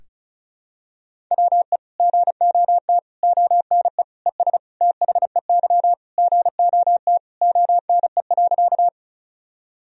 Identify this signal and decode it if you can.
Morse code, heard as a keyed tone, about 34 wpm, 700 Hz. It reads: WE GOT ONE ES THEY GOT ONE.
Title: Sentences from Top 300 Words and Abbreviations 35wpm